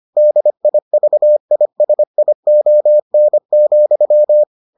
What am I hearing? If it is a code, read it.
Morse code, heard as a keyed tone, about 25 wpm, 600 Hz. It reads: DIVISION,